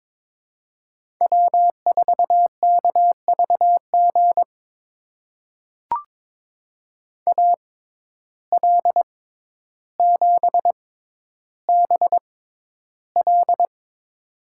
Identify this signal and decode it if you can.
Morse code, heard as a keyed tone, about 22 wpm, spaced as for 11 wpm, 700 Hz. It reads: W4KVG AL7BL
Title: USA Call Signs - 5 Characters - 6x Character Spacing - 22wpm